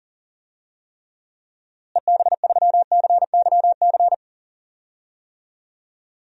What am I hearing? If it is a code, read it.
Morse code, heard as a keyed tone, about 40 wpm, 700 Hz. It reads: EB3CYC
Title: DXCC-100 Call Signs - 6 Characters - Repeated 3x - 40wpm